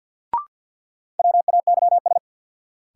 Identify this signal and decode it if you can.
Morse code, heard as a keyed tone, about 47 wpm, 700 Hz. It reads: WAYS
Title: Sets of 5 Words Encore - New Word Drill - 50wpm